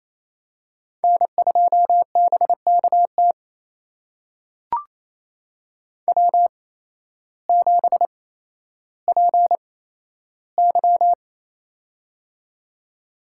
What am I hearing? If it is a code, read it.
Morse code, heard as a keyed tone, about 27 wpm, spaced as for 13 wpm, 700 Hz. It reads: N2BKT W7PY
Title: USA Call Signs - 8x Character Spacing - 28wpm